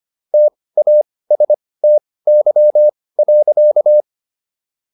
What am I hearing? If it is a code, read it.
Morse code, heard as a keyed tone, about 25 wpm, spaced as for 18 wpm, 600 Hz. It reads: TASTY.